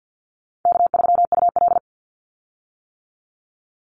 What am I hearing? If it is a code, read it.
Morse code, heard as a keyed tone, about 50 wpm, 700 Hz. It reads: K3UL
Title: Call Signs - 50wpm